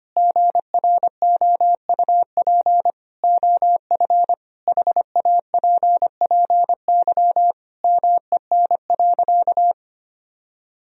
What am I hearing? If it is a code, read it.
Morse code, heard as a keyed tone, about 25 wpm, 700 Hz. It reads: GROUP OF HAPPY MEN.
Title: Sentences from Top 1000 Words - Review Part 1 - 25wpm